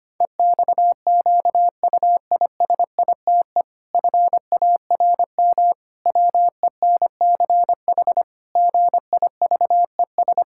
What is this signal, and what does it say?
Morse code, heard as a keyed tone, about 25 wpm, 700 Hz. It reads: EXQUISITE FARM WENCH GIVES
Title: Pangram Challenge v2 - 25wpm